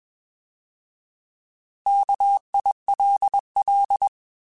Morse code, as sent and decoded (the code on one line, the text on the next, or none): -.- .. .-.. .-..
KILL